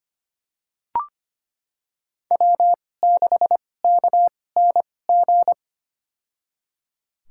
Morse code, heard as a keyed tone, about 25 wpm, spaced as for 18 wpm, 700 Hz. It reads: W6KNG
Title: USA Call Signs - 5 Characters - 2x Character Spacing - 25wpm